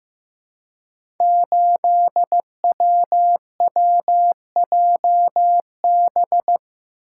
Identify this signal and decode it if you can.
Morse code, heard as a keyed tone, about 15 wpm, 700 Hz. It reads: 8WWJB
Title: Code Groups - 5 Letter:Numbers - 15wpm